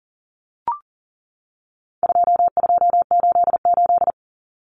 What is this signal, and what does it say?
Morse code, heard as a keyed tone, about 40 wpm, 700 Hz. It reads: 2288
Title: Common Cut Numbers - 4-Digit 40wpm